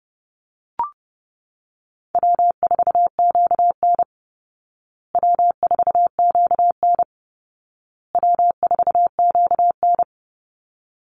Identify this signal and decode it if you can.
Morse code, heard as a keyed tone, about 28 wpm, 700 Hz. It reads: W4QN W4QN W4QN
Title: Easy Call Signs Repeated 3x - 30wpm